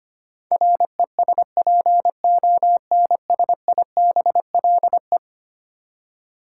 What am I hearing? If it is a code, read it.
Morse code, heard as a keyed tone, about 25 wpm, 700 Hz. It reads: RESPONSIBLE
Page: morse-code-ninja-practice-25wpm